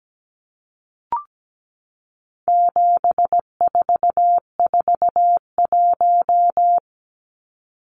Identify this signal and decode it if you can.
Morse code, heard as a keyed tone, about 17 wpm, 700 Hz. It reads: 7441
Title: Numbers - 4-Digit 17wpm